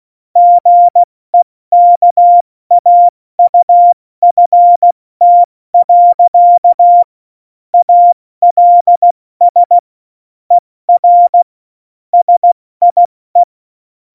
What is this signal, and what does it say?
Morse code, heard as a keyed tone, about 16 wpm, 700 Hz. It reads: GEKAUFT. ALS ER SIE
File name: Morsekurs / Witze / Witz 14. November 2017, 70 F80